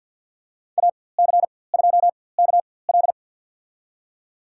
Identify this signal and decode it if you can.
Morse code, heard as a keyed tone, about 47 wpm, spaced as for 23 wpm, 700 Hz. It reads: AC2KL
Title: USA Call Signs - 4x Character Spacing - 50wpm